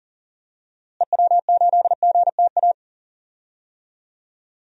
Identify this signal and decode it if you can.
Morse code, heard as a keyed tone, about 38 wpm, 700 Hz. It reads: EW8GTA
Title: DXCC-100 Call Signs - 6 Characters - Repeated 3x - 40wpm